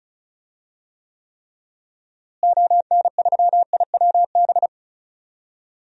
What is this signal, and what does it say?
Morse code, heard as a keyed tone, about 36 wpm, 700 Hz. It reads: ON3IWB
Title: DXCC-100 Call Signs - 6 Characters - Repeated 3x - 35wpm